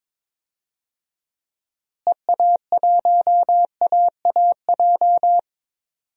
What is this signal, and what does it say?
Morse code, heard as a keyed tone, about 22 wpm, 700 Hz. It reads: EA1AAJ
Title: DXCC-100 Call Signs - 6 Characters - Repeated 2x - 22wpm